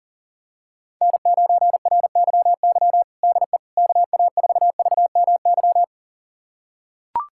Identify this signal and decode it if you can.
Morse code, heard as a keyed tone, about 40 wpm, 700 Hz. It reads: N9RYY DE KA4VMY
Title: USA Callsign DE Callsign 40wpm